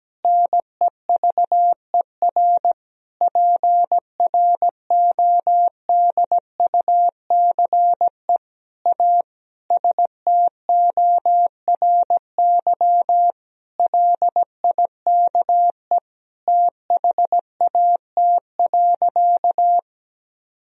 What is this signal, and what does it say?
Morse code, heard as a keyed tone, about 17 wpm, 700 Hz. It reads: NEVER PRODUCE A STORY LIKE THAT.
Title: Sentences from Top 1000 Words - Review Part 3 - 17wpm